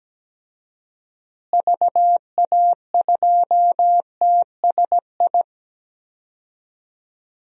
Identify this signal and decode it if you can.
Morse code, heard as a keyed tone, about 17 wpm, 700 Hz. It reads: VA2TSI
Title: DXCC-100 Call Signs - 6 Characters - Repeated 2x - 17wpm